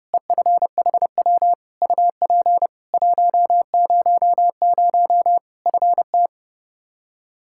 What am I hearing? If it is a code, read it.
Morse code, heard as a keyed tone, about 30 wpm, 700 Hz. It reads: EFHW UP 100 FT
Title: K1USN SST QSO Practice 30wpm